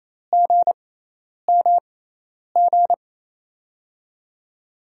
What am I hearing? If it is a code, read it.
Morse code, heard as a keyed tone, about 27 wpm, spaced as for 10 wpm, 700 Hz. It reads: GMG
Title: USA Call Signs - 6x Character Spacing - 28wpm